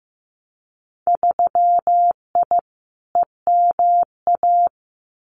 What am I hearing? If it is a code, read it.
Morse code, heard as a keyed tone, about 15 wpm, 700 Hz. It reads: 3I EMA
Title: Winter Field Day Exchanges Encore 15wpm